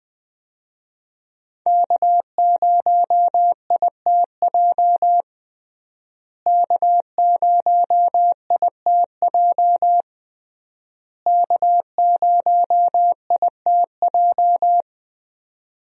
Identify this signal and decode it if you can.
Morse code, heard as a keyed tone, about 20 wpm, 700 Hz. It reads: K0ITJ K0ITJ K0ITJ
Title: USA Call Signs - 5 Characters - Repeated 3x - 20wpm